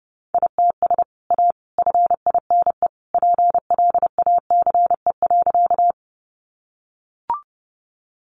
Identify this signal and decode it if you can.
Morse code, heard as a keyed tone, about 28 wpm, 700 Hz. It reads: ITS A FINE PLACE.
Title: Sentences from Top 600 Words 30wpm